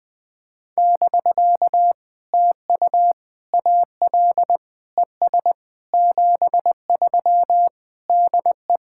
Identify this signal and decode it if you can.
Morse code, heard as a keyed tone, about 21 wpm, 700 Hz. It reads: <BK> TU AL ES 73 DE
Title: POTA QSO Practice - 20wpm